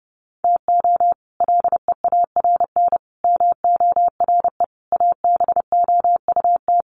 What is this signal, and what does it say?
Morse code, heard as a keyed tone, about 30 wpm, 700 Hz. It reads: TO LEARN MORE ABOUT
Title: Sets of 5 Words Encore2 30wpm